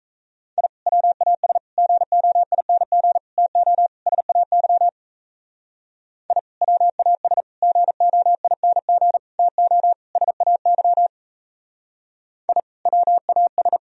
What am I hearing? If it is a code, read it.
Morse code, heard as a keyed tone, about 35 wpm, 700 Hz. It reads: I WAS GOING TO SAY I WAS GOING TO SAY I WAS